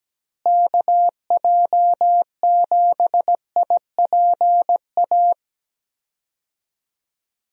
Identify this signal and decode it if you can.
Morse code, heard as a keyed tone, about 17 wpm, 700 Hz. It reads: KJ7IPA
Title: USA Call Signs - 6 Characters - Repeated 2x - 17wpm